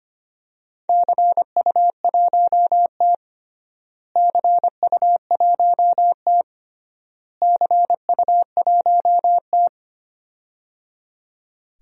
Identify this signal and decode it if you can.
Morse code, heard as a keyed tone, about 25 wpm, 700 Hz. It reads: CU1T CU1T CU1T
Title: DXCC-100 Call Signs - 4 Characters - Repeated 3x - 25wpm